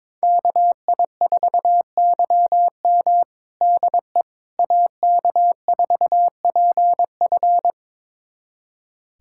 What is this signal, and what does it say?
Morse code, heard as a keyed tone, about 22 wpm, 700 Hz. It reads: KI4YM DE AK4PF
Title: USA Callsign DE Callsign - 5 Characters - 22wpm